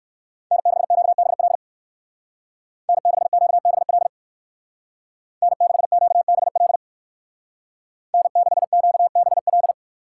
Morse code, heard as a keyed tone, about 59 wpm, 700 Hz. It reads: N6QBL N6QBL N6QBL N6QBL